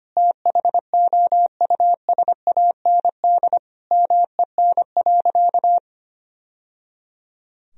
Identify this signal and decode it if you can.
Morse code, heard as a keyed tone, about 25 wpm, 700 Hz. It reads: THOUSAND MEN.
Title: Sentences from Top 1000 Words - Review Part 3 - 25wpm